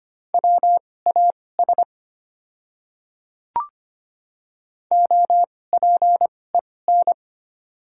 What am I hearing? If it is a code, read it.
Morse code, heard as a keyed tone, about 25 wpm, spaced as for 18 wpm, 690 Hz. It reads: WAS OPEN
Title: 2x Character Spacing - Top 300 Words - 25wpm